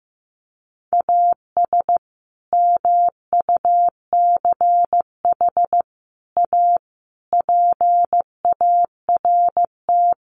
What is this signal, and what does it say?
Morse code, heard as a keyed tone, about 15 wpm, 700 Hz. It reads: AS MUCH A PART
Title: Sets of 5 Words Encore2 15wpm